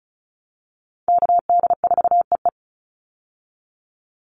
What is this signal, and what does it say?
Morse code, heard as a keyed tone, about 34 wpm, 700 Hz. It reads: KD4EE
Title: Call Signs - 35wpm